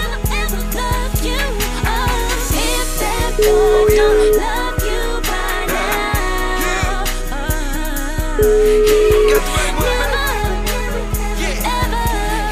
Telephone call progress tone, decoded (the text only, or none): ringback tone